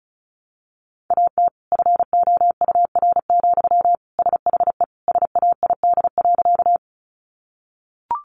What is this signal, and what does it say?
Morse code, heard as a keyed tone, about 34 wpm, 700 Hz. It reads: AT FOUR, SHE SAID.
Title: Sentences from Top 300 Words 35wpm